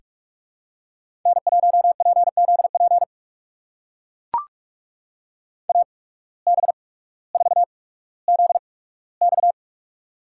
Morse code, heard as a keyed tone, about 41 wpm, spaced as for 19 wpm, 700 Hz. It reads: N1PZP AB4ZX
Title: USA Call Signs - 5 Characters - 8x Character Spacing - 45wpm